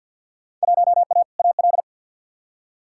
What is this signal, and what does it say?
Morse code, heard as a keyed tone, about 53 wpm, 700 Hz. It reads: 1A AL
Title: Field Day Exchanges - 50wpm